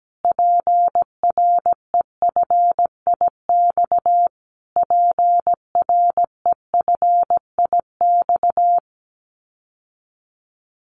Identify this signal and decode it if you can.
Morse code, heard as a keyed tone, about 17 wpm, 700 Hz. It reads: PREFIX PREFIX
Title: Callsign Prefixes for Top 25 DXCC - Repeated 3x - 17wpm